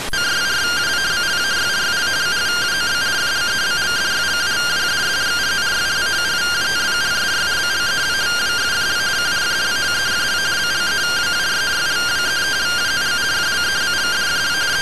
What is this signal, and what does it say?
Two-tone teleprinter signal, about 170 Hz shift.